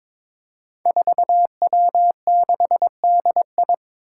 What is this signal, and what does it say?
Morse code, heard as a keyed tone, about 22 wpm, 700 Hz. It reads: 4W6DI
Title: Five Challenging Letter-Number Combinations 22wpm